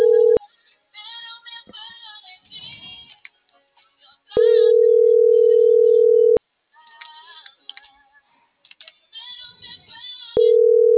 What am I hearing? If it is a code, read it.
Ringback tone (North America), 440 + 480 Hz.